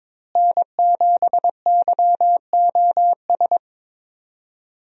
Morse code, heard as a keyed tone, about 22 wpm, 700 Hz. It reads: N7YOS
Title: USA Call Signs - 5 Characters - Repeated 2x - 22wpm